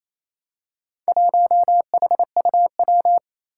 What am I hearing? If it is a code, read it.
Morse code, heard as a keyed tone, about 27 wpm, 700 Hz. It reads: 1HUW
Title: Four Challenging Letter-Number Combinations 28wpm